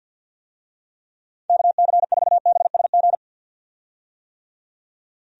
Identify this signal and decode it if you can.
Morse code, heard as a keyed tone, about 52 wpm, 700 Hz. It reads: KC4BSG
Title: USA Call Signs - 6 Characters - 50wpm